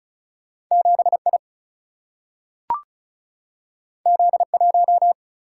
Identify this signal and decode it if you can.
Morse code, heard as a keyed tone, about 36 wpm, 700 Hz. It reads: 7I Z1
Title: Single-Letter Single-Number 35wpm